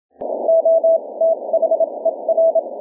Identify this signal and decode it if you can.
Morse code, heard as a keyed tone, about 26 wpm, spaced as for 22 wpm, 660 Hz. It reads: OTHER
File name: 16 Aug 2012 2310 21.020MHz